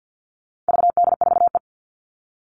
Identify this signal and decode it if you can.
Morse code, heard as a keyed tone, about 48 wpm, 700 Hz. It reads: VD4E
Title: All Cut Numbers - 4-Digit 50wpm